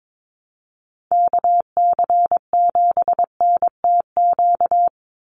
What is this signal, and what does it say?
Morse code, heard as a keyed tone, about 22 wpm, 700 Hz. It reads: KC7NTQ
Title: USA Call Signs 22wpm